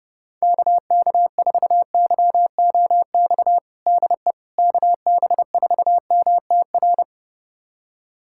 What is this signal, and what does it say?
Morse code, heard as a keyed tone, about 30 wpm, 700 Hz. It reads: KK4YOX DE KB4MTR
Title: USA Callsign DE Callsign 30wpm